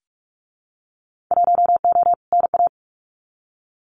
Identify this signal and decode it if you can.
Morse code, heard as a keyed tone, about 46 wpm, 700 Hz. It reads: 1O NA
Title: Winter Field Day Exchanges 45wpm